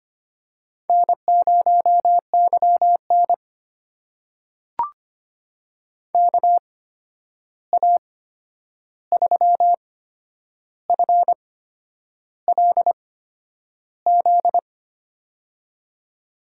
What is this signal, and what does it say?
Morse code, heard as a keyed tone, about 25 wpm, spaced as for 9 wpm, 700 Hz. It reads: N0YN KA3FLZ
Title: USA Call Signs - 8x Character Spacing - 25wpm